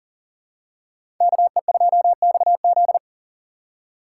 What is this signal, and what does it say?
Morse code, heard as a keyed tone, about 40 wpm, 700 Hz. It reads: KE2XZ
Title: USA Call Signs - 5 Characters - 3x Character Spacing - 40wpm